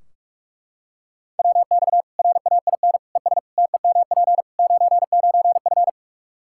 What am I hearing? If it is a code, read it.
Morse code, heard as a keyed tone, about 41 wpm, 700 Hz. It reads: WX RAIN ES TEMP 99F
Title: Practice QSO - Exchange 2 - Info - 45wpm